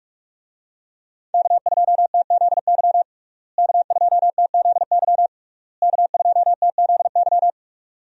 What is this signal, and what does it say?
Morse code, heard as a keyed tone, about 42 wpm, 700 Hz. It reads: K2TZY K2TZY K2TZY
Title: USA Call Signs - 5 Characters - Repeated 3x - 45wpm